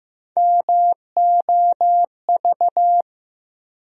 Morse code, heard as a keyed tone, about 15 wpm, 700 Hz. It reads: MOV
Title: USA Call Sign Postfixes - 3 Letters 15wpm